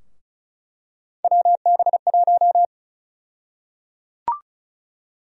Morse code, heard as a keyed tone, about 34 wpm, 700 Hz. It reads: WB1
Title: USA Call Sign Prefixes - 35wpm